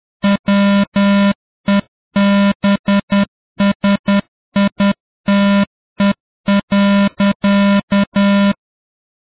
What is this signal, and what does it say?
Morse code, heard as a keyed tone, about 10 wpm, 200 Hz. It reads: WEBSITE.